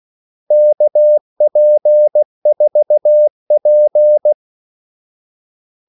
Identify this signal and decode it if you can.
Morse code, heard as a keyed tone, about 16 wpm, 600 Hz. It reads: KP4P
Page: Kaldesignaler 16 wpm